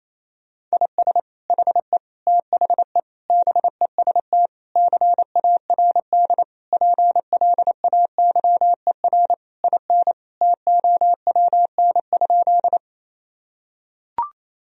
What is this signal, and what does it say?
Morse code, heard as a keyed tone, about 27 wpm, 700 Hz. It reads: IS HE THE BEST CARD PLAYER IN TOWN?
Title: Sentences from Top 1000 Words - Review Part 4 - 28wpm